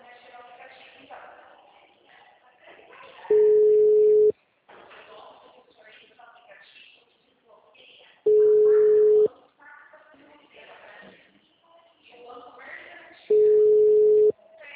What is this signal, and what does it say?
Ringback tone (Europe, 425 Hz), with speech.
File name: out- -2040- - - .25325